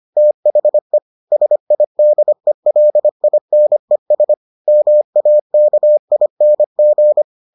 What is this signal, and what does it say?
Morse code, heard as a keyed tone, about 25 wpm, 600 Hz. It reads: THE SIDELINES MAKING